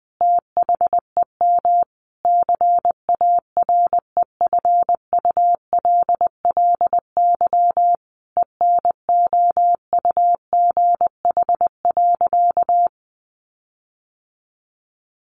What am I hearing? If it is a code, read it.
Morse code, heard as a keyed tone, about 20 wpm, 700 Hz. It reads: THEM CAREFULLY ENOUGH.
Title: Sentences from Top 1300 Words 20wpm